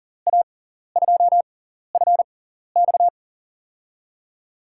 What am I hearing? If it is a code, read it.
Morse code, heard as a keyed tone, about 40 wpm, spaced as for 14 wpm, 700 Hz. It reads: A2FX
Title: USA Call Signs - 5 Characters - 6x Character Spacing - 40wpm